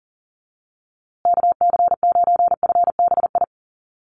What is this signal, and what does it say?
Morse code, heard as a keyed tone, about 40 wpm, 700 Hz. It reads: KC9FBI
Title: USA Call Signs Repeated 2x 40wpm